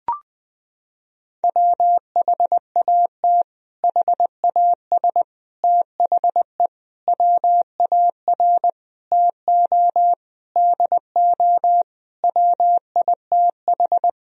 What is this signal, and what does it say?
Morse code, heard as a keyed tone, about 20 wpm, 700 Hz. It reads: WHAT HAS THE WAR TO DO WITH
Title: Sentences from Top 1000 Words - Review Part 2 - 20wpm